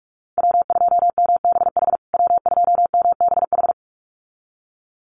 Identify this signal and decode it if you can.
Morse code, heard as a keyed tone, about 44 wpm, 690 Hz. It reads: W2MBH W2MBH
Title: USA Call Signs Repeated 2x 45wpm